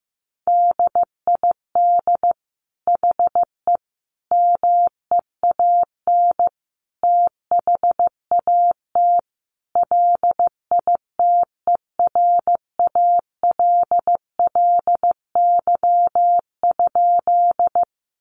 Morse code, heard as a keyed tone, about 15 wpm, 700 Hz. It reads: DID HE MEAN THAT LITERALLY?
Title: Sentences from Top 1400 Words 15wpm